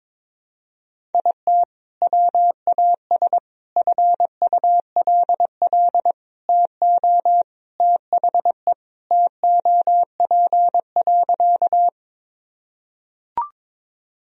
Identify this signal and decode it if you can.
Morse code, heard as a keyed tone, about 22 wpm, 700 Hz. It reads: IT WAS FULL TO THE TOP.
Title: Sentences from Top 1000 Words - Review Part 3 - 22wpm